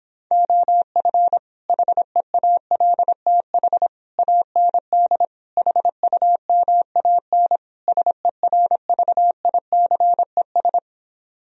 Morse code, heard as a keyed tone, about 26 wpm, 700 Hz. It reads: OF HEALTH AND HUMAN SERVICES